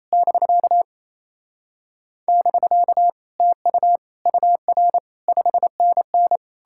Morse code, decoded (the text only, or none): <BK> <BK> TU UR 5NN